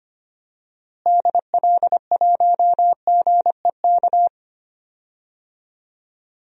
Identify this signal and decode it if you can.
Morse code, heard as a keyed tone, about 25 wpm, 700 Hz. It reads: DL1GEK
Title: DXCC-100 Call Signs - 6 Characters - Repeated 2x - 25wpm